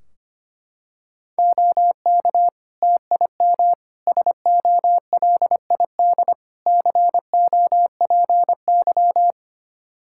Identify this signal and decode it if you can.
Morse code, heard as a keyed tone, about 25 wpm, 700 Hz. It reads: OK TIM SOLID COPY
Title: Practice QSO - Exchange 3 - More Info - 25wpm